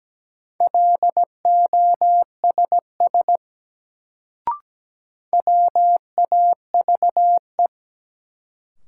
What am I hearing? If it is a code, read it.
Morse code, heard as a keyed tone, about 17 wpm, 700 Hz. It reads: LOSS WAVE
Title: Sentences from Top 1400 Words - New Word Drill - 17wpm